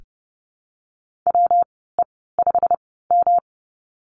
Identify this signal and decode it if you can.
Morse code, heard as a keyed tone, about 30 wpm, spaced as for 17 wpm, 700 Hz. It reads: WE5M
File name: Lesson 19g - Letter w - Call Signs 30wpm